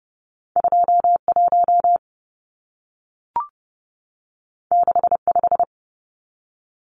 Morse code, heard as a keyed tone, about 30 wpm, 700 Hz. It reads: 21 65
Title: Numbers - 2-Digit 30wpm